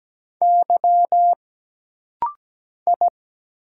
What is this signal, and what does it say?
Morse code, heard as a keyed tone, about 17 wpm, 700 Hz. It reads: Y I
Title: Single Letters - Rapid-Fire 17wpm